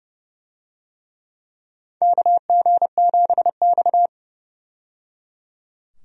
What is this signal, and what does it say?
Morse code, heard as a keyed tone, about 30 wpm, 700 Hz. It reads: KG7X
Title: USA Call Signs - 4 Characters - 30wpm